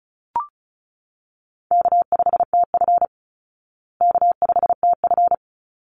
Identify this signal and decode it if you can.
Morse code, heard as a keyed tone, about 34 wpm, 700 Hz. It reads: K5TF K5TF
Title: Easy Call Signs Repeated 2x - 35wpm